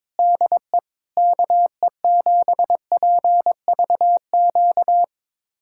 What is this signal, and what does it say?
Morse code, heard as a keyed tone, about 22 wpm, 700 Hz. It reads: DE KE7PVQ
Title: USA Callsign DE Callsign 22wpm